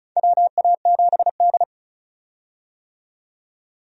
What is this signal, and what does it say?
Morse code, heard as a keyed tone, about 34 wpm, 700 Hz. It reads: WA7D
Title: USA Call Signs - 4 Characters - 35wpm